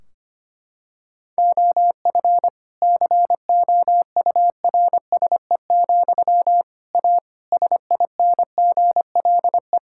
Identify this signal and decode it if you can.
Morse code, heard as a keyed tone, about 25 wpm, 700 Hz. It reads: OF COURSE, A SINGLE
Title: Sentences from Top 1600 Words 25wpm